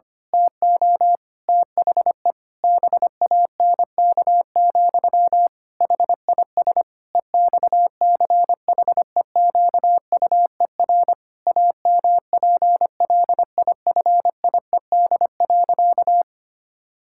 Morse code, heard as a keyed tone, about 25 wpm, 700 Hz. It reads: TO THE BANK, HIS EXCHEQUER AMPLIFIED.
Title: Pangram Challenge v2 - 25wpm